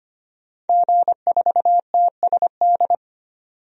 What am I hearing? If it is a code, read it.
Morse code, heard as a keyed tone, about 25 wpm, 690 Hz. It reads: G4TSD
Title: DXCC-100 Call Signs - 5 Characters - 25wpm